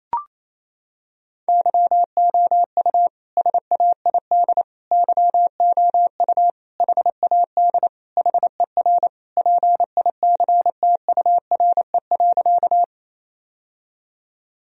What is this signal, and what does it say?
Morse code, heard as a keyed tone, about 28 wpm, 700 Hz. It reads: YOU SAID YOU HAD HER PICTURE.
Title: Sentences from Top 1000 Words - Review Part 1 - 28wpm